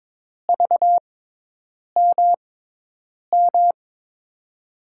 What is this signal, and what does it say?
Morse code, heard as a keyed tone, about 22 wpm, spaced as for 8 wpm, 700 Hz. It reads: VMM
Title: USA Call Signs - 6 Characters - 6x Character Spacing - 22wpm